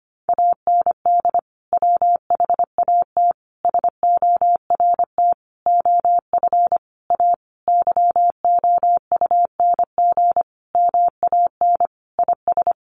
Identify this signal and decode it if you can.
Morse code, heard as a keyed tone, about 25 wpm, 700 Hz. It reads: AND WHAT SORT OF A YOUNG MAN IS
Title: Sentences from Top 1300 Words 25wpm